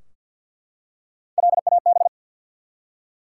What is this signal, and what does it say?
Morse code, heard as a keyed tone, about 51 wpm, 700 Hz. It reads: LAB
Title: Three Letter Words - 50wpm